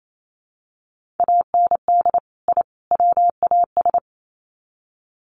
Morse code, heard as a keyed tone, about 28 wpm, 700 Hz. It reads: AND I WAS
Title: Sets of 3 Words 28wpm